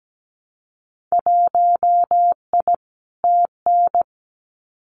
Morse code, heard as a keyed tone, about 17 wpm, 690 Hz. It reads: 1I TN
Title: Winter Field Day Exchanges 17wpm